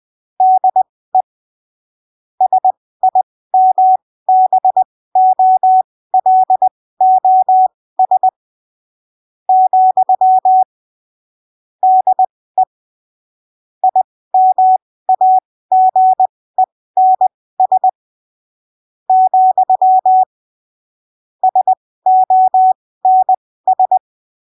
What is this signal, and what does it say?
Morse code, heard as a keyed tone, about 20 wpm, spaced as for 15 wpm, 750 Hz. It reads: DE SIMBOLOS , DE IMAGENS , SONS
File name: arquivos / texto / texto 18 20ppm